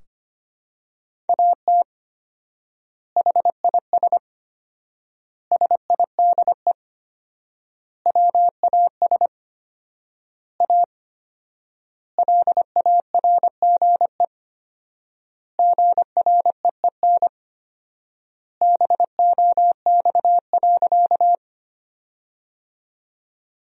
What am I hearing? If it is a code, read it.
Morse code, heard as a keyed tone, about 25 wpm, 700 Hz. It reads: AT HIS SIDE WAS A LARGE GREEN BOX.
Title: 4x Word-Spacing - Sentences from Top 500 Words - 25wpm